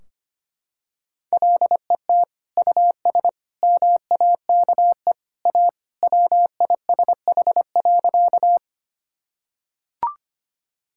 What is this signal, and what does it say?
Morse code, heard as a keyed tone, about 25 wpm, 700 Hz. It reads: LET US MAKE A WISH.
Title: Sentences from Top 1600 Words 25wpm